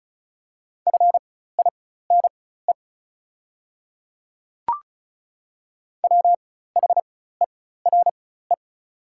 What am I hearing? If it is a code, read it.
Morse code, heard as a keyed tone, about 34 wpm, spaced as for 16 wpm, 690 Hz. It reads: FINE WHERE
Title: 4x Character Spacing - Top 500 Words - 35wpm